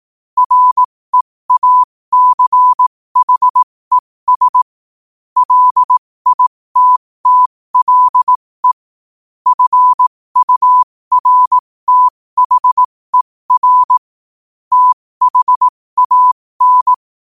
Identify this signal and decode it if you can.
Morse code, heard as a keyed tone, about 18 wpm, spaced as for 15 wpm, 1000 Hz. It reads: REACHES LITTLE FURTHER THAN
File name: qotdfiles / 15wpm